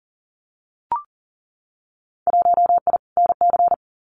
Morse code, heard as a keyed tone, about 42 wpm, 700 Hz. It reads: 1I NC
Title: Winter Field Day Exchanges 40wpm